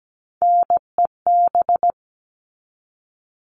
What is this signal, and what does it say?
Morse code, heard as a keyed tone, about 17 wpm, 700 Hz. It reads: NEB
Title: All Cut Numbers - 3-Digit 17wpm